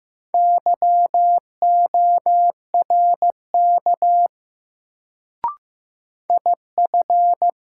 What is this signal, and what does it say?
Morse code, heard as a keyed tone, about 15 wpm, 700 Hz. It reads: YORK IF
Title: Sets of 2 Words - New Word Drill - 15wpm